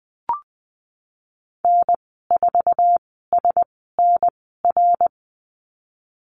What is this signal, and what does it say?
Morse code, heard as a keyed tone, about 20 wpm, spaced as for 15 wpm, 700 Hz. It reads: N4SNR
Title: 140 Lesson 23g - Letter f - Call Signs 20wpm